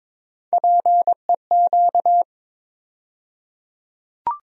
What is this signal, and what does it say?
Morse code, heard as a keyed tone, about 22 wpm, 700 Hz. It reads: PEQ
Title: USA Call Sign Postfixes - 3 Letters 22wpm